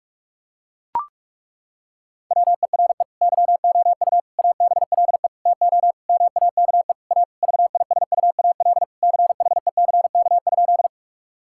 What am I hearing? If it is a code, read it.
Morse code, heard as a keyed tone, about 41 wpm, 700 Hz. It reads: WERE YOU ABLE TO MAKE A VISUAL CHECK?
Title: Sentences from Top 1000 Words - Review Part 6 - 45wpm